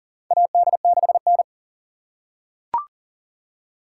Morse code, heard as a keyed tone, about 40 wpm, 700 Hz. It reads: AD6N
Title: USA Call Signs - 4 Characters - 3x Character Spacing - 40wpm